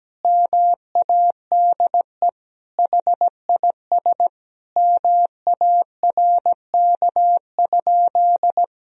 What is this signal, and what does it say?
Morse code, heard as a keyed tone, about 17 wpm, 700 Hz. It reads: MADE HIS MARK?
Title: Sentences from Top 1000 Words - Review Part 1 - 17wpm